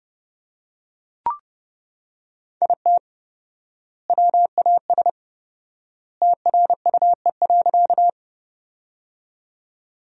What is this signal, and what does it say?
Morse code, heard as a keyed tone, about 32 wpm, 700 Hz. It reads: IT WAS TRUE.
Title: 4x Word-Spacing - Sentences from Top 400 Words - 30wpm